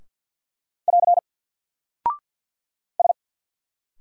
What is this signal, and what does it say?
Morse code, heard as a keyed tone, about 50 wpm, 700 Hz. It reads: <AR> S
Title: Single Character (Letter:Number:Punctuation:Prosign) 50wpm